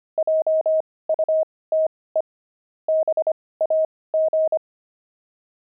Morse code, heard as a keyed tone, about 25 wpm, spaced as for 18 wpm, 630 Hz. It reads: JUTE BAG
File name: pangram-fs-18-cs-25-sixty